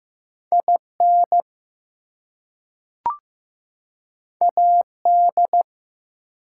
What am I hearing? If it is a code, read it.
Morse code, heard as a keyed tone, about 15 wpm, 700 Hz. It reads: IN AD
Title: Two Letter Words - 15wpm